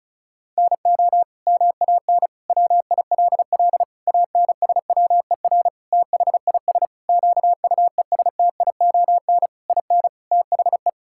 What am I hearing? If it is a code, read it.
Morse code, heard as a keyed tone, about 34 wpm, 700 Hz. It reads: NO MAN WILL ANSWER THIS QUESTION IN THE
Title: Sentences from Top 1000 Words - Review Part 6 - 35wpm